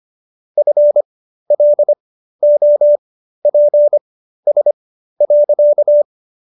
Morse code, heard as a keyed tone, about 25 wpm, spaced as for 13 wpm, 590 Hz. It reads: FLOPS.